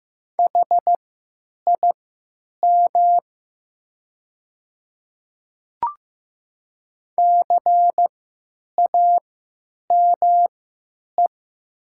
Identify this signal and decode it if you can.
Morse code, heard as a keyed tone, about 15 wpm, spaced as for 9 wpm, 700 Hz. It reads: HIM CAME
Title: 3x Character Spacing - Top 300 Words - 15wpm